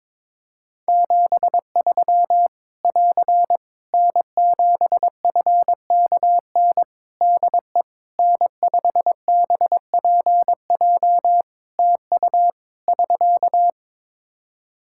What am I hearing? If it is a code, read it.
Morse code, heard as a keyed tone, about 21 wpm, 700 Hz. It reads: 73 <AR> N7FKN DE N5BPJ TU <SK>
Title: Practice QSO - Exchange 4 - Ending - 22wpm